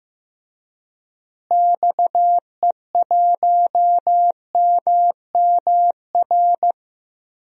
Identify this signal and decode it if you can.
Morse code, heard as a keyed tone, about 15 wpm, 700 Hz. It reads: XE1MMR
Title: DXCC-100 Call Signs - 6 Characters - Repeated 3x - 15wpm